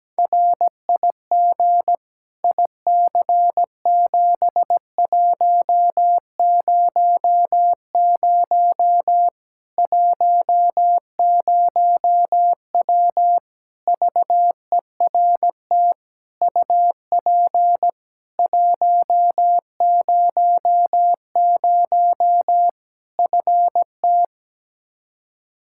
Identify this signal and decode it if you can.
Morse code, heard as a keyed tone, about 17 wpm, 700 Hz. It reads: RIG IC7100 10W VERT UP 100 FT
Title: Practice QSO - Exchange 2 - Info - 17wpm